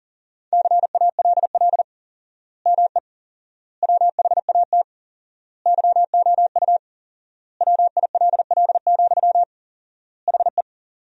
Morse code, heard as a keyed tone, about 37 wpm, 700 Hz. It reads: CALL ME WHAT YOU WILL, HE
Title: 4x Word-Spacing - Sentences from Top 200 Words - 40wpm